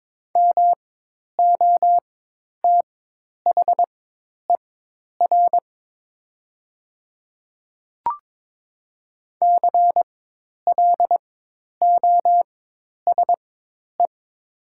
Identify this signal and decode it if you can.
Morse code, heard as a keyed tone, about 22 wpm, spaced as for 10 wpm, 700 Hz. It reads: MOTHER CLOSE
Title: 4x Character Spacing - Top 500 Words - 22wpm